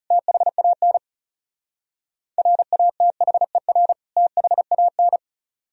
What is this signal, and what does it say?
Morse code, heard as a keyed tone, about 36 wpm, 700 Hz. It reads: THAN RATHER THAN
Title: Speed-Racing - Sets of 2 Words 30wpm